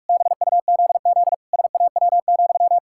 Morse code, heard as a keyed tone, about 41 wpm, 700 Hz. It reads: BUZZ SAW,
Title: Pangram Challenge v2 - 45wpm